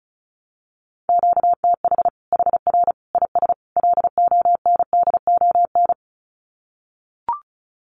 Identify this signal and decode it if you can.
Morse code, heard as a keyed tone, about 34 wpm, 700 Hz. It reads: QTH HR IS LONDON
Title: Common QSO Phrases 35wpm